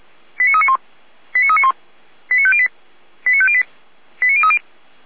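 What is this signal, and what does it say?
Five-tone selective call (ZVEI), heard as quick runs of five tones, tones of about 70 ms. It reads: 89391 89391 89508 89508 89030
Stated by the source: ZVEI five-tone selective call received by SDR.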